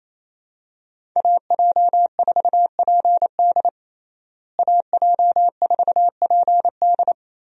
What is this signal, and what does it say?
Morse code, heard as a keyed tone, about 28 wpm, 700 Hz. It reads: AJ4PD AJ4PD
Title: USA Call Signs - 5 Characters - Repeated 2x - 28wpm